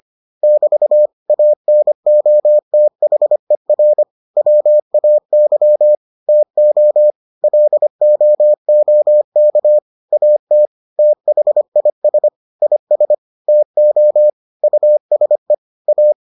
Morse code, heard as a keyed tone, about 25 wpm, 600 Hz. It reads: <BT> ANOTHER WAY TO LOOK AT THIS IS TO USE A